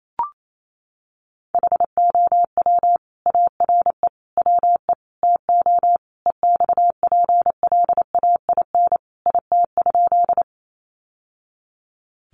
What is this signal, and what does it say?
Morse code, heard as a keyed tone, about 28 wpm, 700 Hz. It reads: HOW ARE WE TO EXPLAIN IT?
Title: Sentences from Top 1100 Words 28wpm